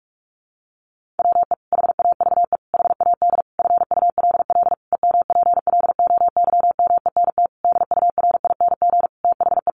Morse code, heard as a keyed tone, about 44 wpm, 700 Hz. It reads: WE HAVE HAD FULL EMPLOYMENT DURING THE
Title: Sentences from Top 1500 Words 45wpm